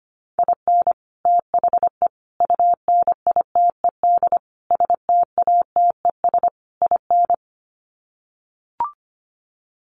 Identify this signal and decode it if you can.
Morse code, heard as a keyed tone, about 25 wpm, 700 Hz. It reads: IN THE UNITED STATES IN
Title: Sets of 5 Words 25wpm